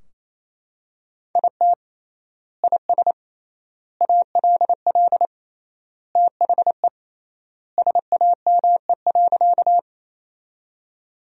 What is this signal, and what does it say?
Morse code, heard as a keyed tone, about 27 wpm, 700 Hz. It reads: IT IS ALL THE SAME.
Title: 3x Word-Spacing - Sentences from Top 200 Words - 28wpm